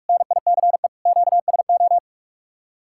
Morse code, heard as a keyed tone, about 60 wpm, spaced as for 52 wpm, 700 Hz. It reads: NICE QSO
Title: Speed-Racing Common QSO Phrases Encore - 45wpm